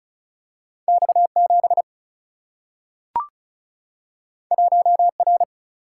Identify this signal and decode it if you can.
Morse code, heard as a keyed tone, about 36 wpm, 700 Hz. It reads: X7 1R
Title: Single-Letter Single-Number 35wpm